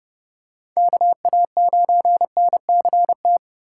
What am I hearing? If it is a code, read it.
Morse code, heard as a keyed tone, about 30 wpm, 700 Hz. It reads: KA9NCT
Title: USA Call Signs - 6 Characters - 2x Character Spacing - 30wpm